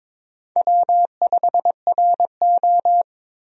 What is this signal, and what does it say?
Morse code, heard as a keyed tone, about 21 wpm, 700 Hz. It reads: W5RO
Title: USA Call Signs - 4 Characters - Repeated 2x - 22wpm